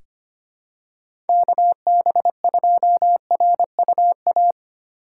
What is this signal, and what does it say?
Morse code, heard as a keyed tone, about 25 wpm, 700 Hz. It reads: KB2RUA
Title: USA Call Signs - 2x Character Spacing - 25wpm